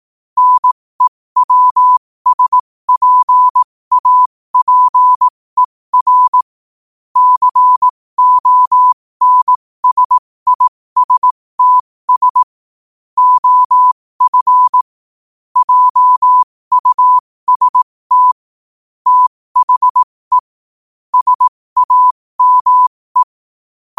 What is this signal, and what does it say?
Morse code, heard as a keyed tone, about 18 wpm, spaced as for 15 wpm, 1000 Hz. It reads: NEWSPAPER CONSISTS OF JUST THE SAME
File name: qotdfiles / 15wpm